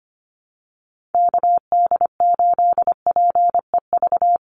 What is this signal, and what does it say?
Morse code, heard as a keyed tone, about 25 wpm, 700 Hz. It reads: KD8PEV
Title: USA Call Signs 25wpm